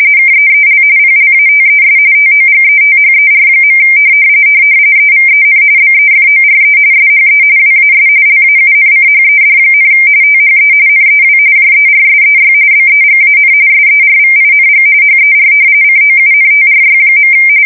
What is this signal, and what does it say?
RTTY, about 45.45 baud. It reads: KEEL PLATES TOGETHER. EVEN BEFORE SHE WAS LAUNCHED, THE NEW JERSEY BEGAN SERVING HER COUNTRY.
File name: fsk170 / battleshipnj